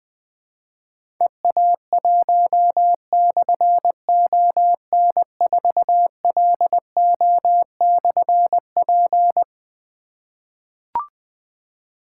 Morse code, heard as a keyed tone, about 20 wpm, 700 Hz. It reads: EA1/ON4LO/P
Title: Difficult Call Signs 20wpm